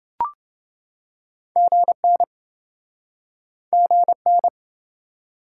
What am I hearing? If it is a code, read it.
Morse code, heard as a keyed tone, about 29 wpm, 700 Hz. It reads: GN GN